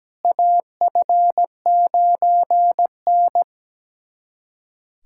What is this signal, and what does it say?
Morse code, heard as a keyed tone, about 17 wpm, 700 Hz. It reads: AF9N
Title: USA Callsign DE Callsign - 4 Characters - 17wpm